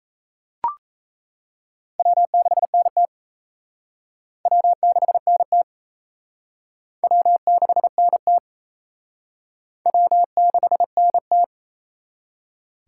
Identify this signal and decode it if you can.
Morse code, heard as a keyed tone, about 33 wpm, 700 Hz. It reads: W6NT W6NT W6NT W6NT
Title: Speed Racing - USA 1x2 Call Signs 28wpm